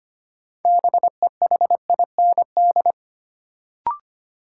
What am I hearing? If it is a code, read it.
Morse code, heard as a keyed tone, about 25 wpm, 700 Hz. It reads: BEHIND
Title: Sentences from Top 500 Words - New Word Drill - 25wpm